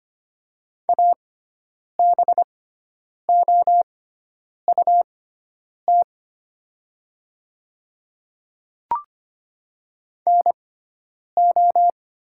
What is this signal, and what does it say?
Morse code, heard as a keyed tone, about 25 wpm, spaced as for 9 wpm, 700 Hz. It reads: ABOUT NO
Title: 6x Character Spacing - Top 100 Words - 25wpm